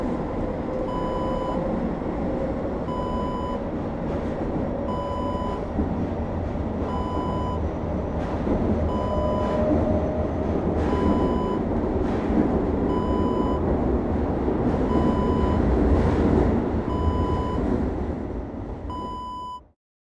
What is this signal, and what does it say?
There is no code.